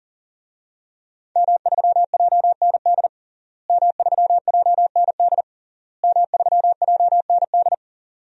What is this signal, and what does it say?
Morse code, heard as a keyed tone, about 40 wpm, 700 Hz. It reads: M3JND M3JND M3JND
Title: DXCC-100 Call Signs - 5 Characters - Repeated 3x - 40wpm